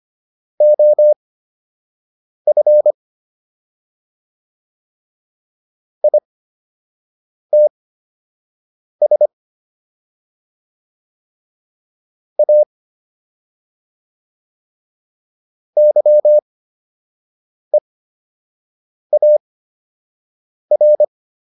Morse code, heard as a keyed tone, about 25 wpm, spaced as for 6 wpm, 600 Hz. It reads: OF ITS A YEAR